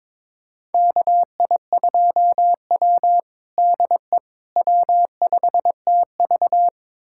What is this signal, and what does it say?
Morse code, heard as a keyed tone, about 22 wpm, 700 Hz. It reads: KI2W DE W5TV
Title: USA Callsign DE Callsign - 4 Characters - 22wpm